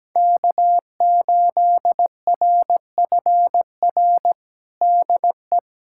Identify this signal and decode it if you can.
Morse code, heard as a keyed tone, about 17 wpm, 700 Hz. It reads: K8RFR DE
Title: USA Callsign DE Callsign - 5 Characters - 17wpm